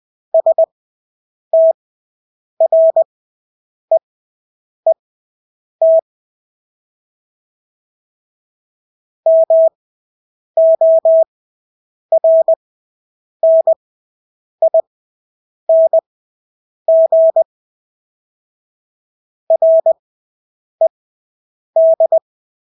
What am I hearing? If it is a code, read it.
Morse code, heard as a keyed tone, about 20 wpm, spaced as for 8 wpm, 650 Hz. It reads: STREET MORNING RED
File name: Engelske ord 20/8 wpm